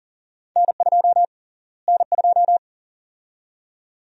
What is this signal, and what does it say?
Morse code, heard as a keyed tone, about 40 wpm, 700 Hz. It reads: N2 N2
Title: USA Call Sign Prefixes - Repeated 2x - 40wpm